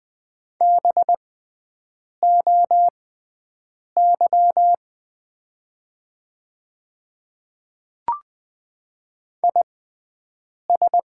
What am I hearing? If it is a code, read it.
Morse code, heard as a keyed tone, about 20 wpm, spaced as for 7 wpm, 700 Hz. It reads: BOY IS
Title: 6x Character Spacing - Top 200 Words - 20wpm